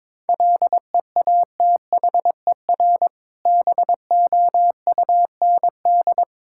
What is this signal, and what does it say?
Morse code, heard as a keyed tone, about 22 wpm, 690 Hz. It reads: LEATHER BOUND
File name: Difficult Words with Hints - 22wpm